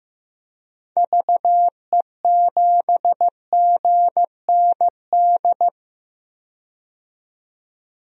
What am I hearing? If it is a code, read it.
Morse code, heard as a keyed tone, about 15 wpm, 700 Hz. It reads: VE7GND
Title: DXCC-100 Call Signs - 6 Characters - 15wpm